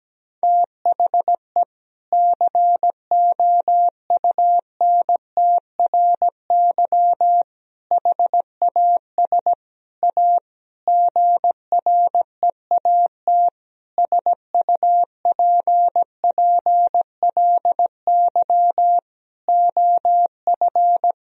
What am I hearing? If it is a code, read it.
Morse code, heard as a keyed tone, about 17 wpm, 700 Hz. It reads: THE COUNTRY HAS A GREAT SUPPLY OF
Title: Sentences from Top 1000 Words - Review Part 4 - 17wpm